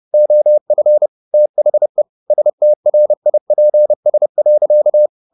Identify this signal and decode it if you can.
Morse code, heard as a keyed tone, about 30 wpm, 600 Hz. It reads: OF THE STRIPS.
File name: the old man and the sea / the old man and the sea 1 30wpm